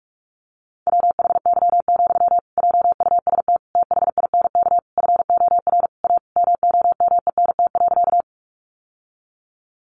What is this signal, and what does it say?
Morse code, heard as a keyed tone, about 44 wpm, 700 Hz. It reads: WHY, JUST THINK FOR A MOMENT.
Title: Sentences from Top 1000 Words 45wpm